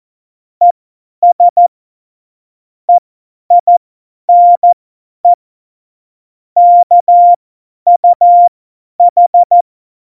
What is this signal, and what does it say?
Morse code, heard as a keyed tone, about 14 wpm, spaced as for 10 wpm, 700 Hz. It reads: ES EINE KUH
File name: Morsekurs / Witze / Witz 15. Dezember 2020, 50 F70